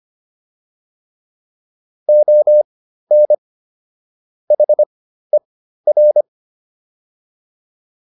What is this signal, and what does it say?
Morse code, heard as a keyed tone, about 25 wpm, spaced as for 13 wpm, 600 Hz. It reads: ON HER